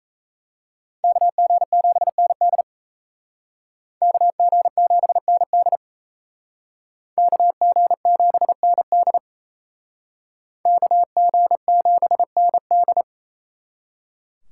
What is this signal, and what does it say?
Morse code, heard as a keyed tone, about 34 wpm, 700 Hz. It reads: KG7ND KG7ND KG7ND KG7ND
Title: Speed Racing - USA Call Signs 28wpm